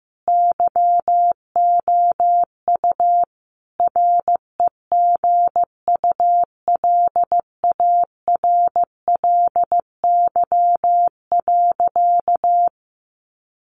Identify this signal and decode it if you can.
Morse code, heard as a keyed tone, about 15 wpm, 700 Hz. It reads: YOU REGULARLY.
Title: Sentences from Top 1400 Words 15wpm